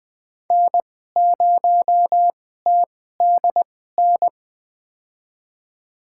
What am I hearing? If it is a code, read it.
Morse code, heard as a keyed tone, about 20 wpm, spaced as for 15 wpm, 700 Hz. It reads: N0TDN
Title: USA Call Signs - 2x Character Spacing - 20wpm